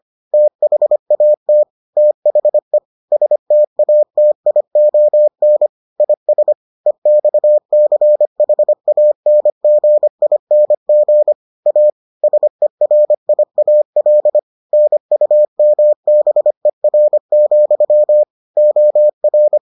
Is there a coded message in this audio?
Yes: Morse code from 0.3 to 19.6 s, heard as a keyed tone, about 25 wpm, 600 Hz. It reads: THAT THE STATION IS EXCHANGING A SERIAL NUMBER, OR